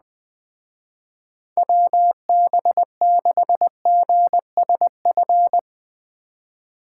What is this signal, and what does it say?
Morse code, heard as a keyed tone, about 20 wpm, 700 Hz. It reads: WB6GSF
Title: Speed Racing - USA Call Signs 17wpm